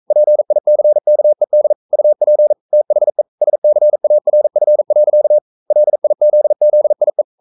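Morse code, heard as a keyed tone, about 42 wpm, 600 Hz. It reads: PICKED UP THE SCARF. LIZZIE
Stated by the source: Morse code sending of an English novel at 42 wpm.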